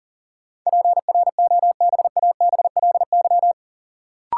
Morse code, heard as a keyed tone, about 40 wpm, 700 Hz. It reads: PROBABLY
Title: Sentences from Top 700 Words - New Word Drill - 40wpm